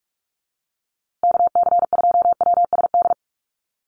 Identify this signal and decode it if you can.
Morse code, heard as a keyed tone, about 44 wpm, 700 Hz. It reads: KC2WSD
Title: USA Call Signs Repeated 3x 45wpm